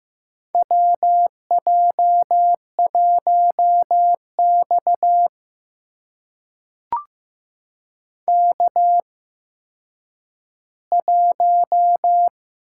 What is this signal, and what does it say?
Morse code, heard as a keyed tone, about 15 wpm, spaced as for 9 wpm, 700 Hz. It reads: WJ1X K1
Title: USA Call Signs - 4 Characters - 8x Character Spacing - 15wpm